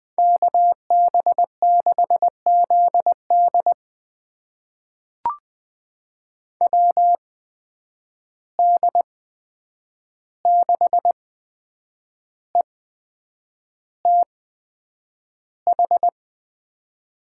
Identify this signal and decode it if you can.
Morse code, heard as a keyed tone, about 20 wpm, spaced as for 8 wpm, 700 Hz. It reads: KB6ZD WD6ETH
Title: USA Call Signs - 8x Character Spacing - 20wpm